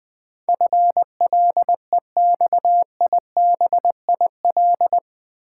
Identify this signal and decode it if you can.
Morse code, heard as a keyed tone, about 20 wpm, 700 Hz. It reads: FLEXIBIL
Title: Build A Word - 4 Letters At A Time - 11-Letter Words - 20wpm